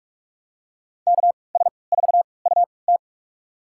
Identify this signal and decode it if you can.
Morse code, heard as a keyed tone, about 47 wpm, spaced as for 26 wpm, 700 Hz. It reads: KS4UT
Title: USA Call Signs - 5 Characters - 3x Character Spacing - 45wpm